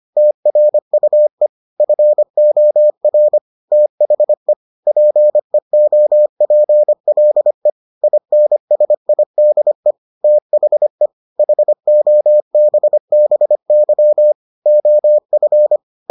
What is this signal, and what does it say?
Morse code, heard as a keyed tone, about 25 wpm, 600 Hz. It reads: TRUE FOR THE PEOPLE INSIDE THE HOBBY OF